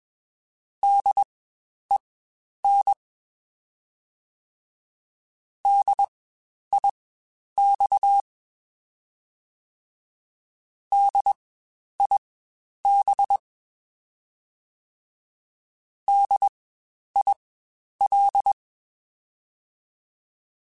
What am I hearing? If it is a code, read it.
Morse code, heard as a keyed tone, about 21 wpm, spaced as for 10 wpm, 800 Hz. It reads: DEN DIX DIB DIL
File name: Spaced Alphas / Spaced Alphas mp3s / Ds sp @20